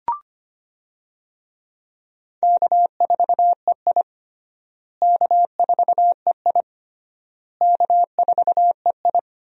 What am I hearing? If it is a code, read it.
Morse code, heard as a keyed tone, about 25 wpm, 700 Hz. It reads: K4EI K4EI K4EI
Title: USA Call Signs - 4 Characters - Repeated 3x - 25wpm